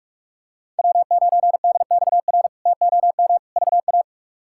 Morse code, heard as a keyed tone, about 46 wpm, 700 Hz. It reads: W9DXR TOM VA
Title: K1USN SST QSO Practice - 45wpm